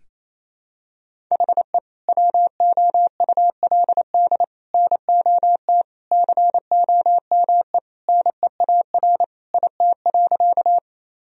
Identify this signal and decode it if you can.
Morse code, heard as a keyed tone, about 28 wpm, 700 Hz. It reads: HE WOULD NOT COME NEAR IT.
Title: Sentences from Top 1000 Words - Review Part 1 - 28wpm